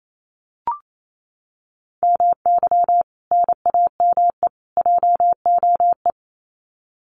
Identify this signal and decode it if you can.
Morse code, heard as a keyed tone, about 28 wpm, 700 Hz. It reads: MY NAME JOE
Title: Common QSO Phrases 28wpm